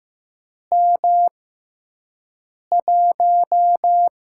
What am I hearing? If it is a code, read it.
Morse code, heard as a keyed tone, about 15 wpm, spaced as for 5 wpm, 700 Hz. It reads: M1
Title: USA Call Signs - 4 Characters - 6x Character Spacing - 15wpm